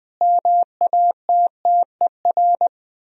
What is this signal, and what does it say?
Morse code, heard as a keyed tone, about 20 wpm, 700 Hz. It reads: MATTER
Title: Speed-Racing - Sets of 5 Words 17wpm